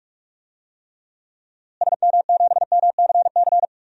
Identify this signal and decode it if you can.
Morse code, heard as a keyed tone, about 41 wpm, 700 Hz. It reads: SM7MCC
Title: DXCC-100 Call Signs - 6 Characters - Repeated 2x - 45wpm